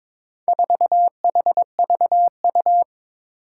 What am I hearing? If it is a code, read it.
Morse code, heard as a keyed tone, about 22 wpm, 700 Hz. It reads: 4HVU
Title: Four Challenging Letter-Number Combinations 22wpm